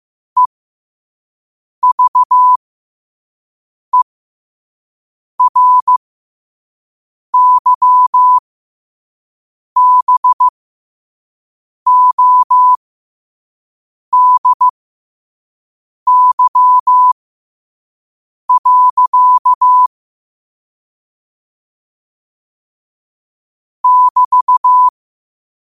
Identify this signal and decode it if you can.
Morse code, heard as a keyed tone, about 15 wpm, spaced as for 5 wpm, 1000 Hz. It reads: EVERYBODY. <BT>
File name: qotdfiles / 5wpm